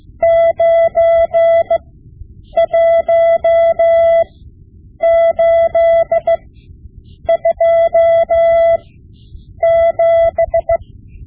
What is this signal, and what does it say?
Morse code, heard as a keyed tone, about 16 wpm, spaced as for 8 wpm, 670 Hz. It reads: 91827
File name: Morse Lessons / Morse Lessons / Tape-5B